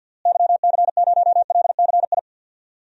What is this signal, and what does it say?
Morse code, heard as a keyed tone, about 46 wpm, 700 Hz. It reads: YC0LCI
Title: DXCC-100 Call Signs - 6 Characters - Repeated 3x - 50wpm